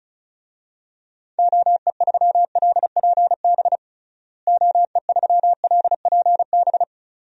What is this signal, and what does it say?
Morse code, heard as a keyed tone, about 36 wpm, 700 Hz. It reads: OE3LPB OE3LPB
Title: DXCC-100 Call Signs - 6 Characters - Repeated 2x - 35wpm